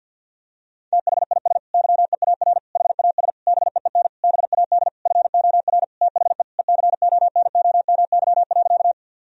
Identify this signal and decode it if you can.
Morse code, heard as a keyed tone, about 51 wpm, 700 Hz. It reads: THIS YEAR HAS BEEN BAD FOR THE ECONOMY.
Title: Sentences from Top 1000 Words - Review Part 3 - 50wpm